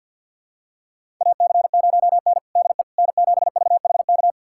Morse code, heard as a keyed tone, about 50 wpm, 700 Hz. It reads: AC0N DE N7VHK
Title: USA Callsign DE Callsign 50wpm